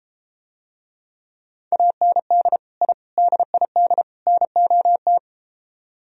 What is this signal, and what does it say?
Morse code, heard as a keyed tone, about 33 wpm, 700 Hz. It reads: AND I DID NOT